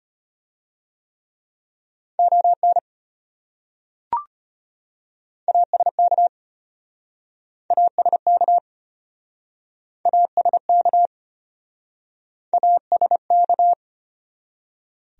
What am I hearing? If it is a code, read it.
Morse code, heard as a keyed tone, about 33 wpm, 700 Hz. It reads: ON ASK ASK ASK ASK